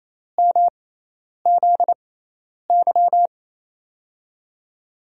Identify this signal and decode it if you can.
Morse code, heard as a keyed tone, about 28 wpm, spaced as for 10 wpm, 700 Hz. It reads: MZY
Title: USA Call Signs - 6 Characters - 6x Character Spacing - 28wpm